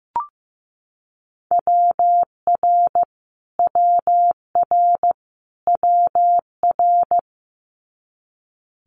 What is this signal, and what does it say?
Morse code, heard as a keyed tone, about 15 wpm, 700 Hz. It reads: WR WR WR
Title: Multi-letter Phonemes Repeated 3x 15wpm